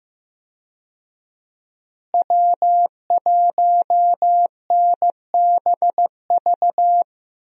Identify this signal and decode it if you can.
Morse code, heard as a keyed tone, about 15 wpm, 700 Hz. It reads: W1NBV
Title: USA Call Signs - 5 Characters - Repeated 2x - 15wpm